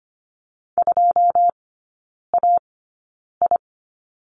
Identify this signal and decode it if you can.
Morse code, heard as a keyed tone, about 25 wpm, spaced as for 9 wpm, 700 Hz. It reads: 2AI
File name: Lesson 17e - Letter u - Three Characters 25wpm